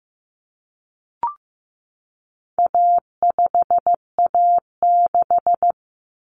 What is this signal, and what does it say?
Morse code, heard as a keyed tone, about 15 wpm, 700 Hz. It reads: A5A6
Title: Common Cut Numbers - 4-Digit 15wpm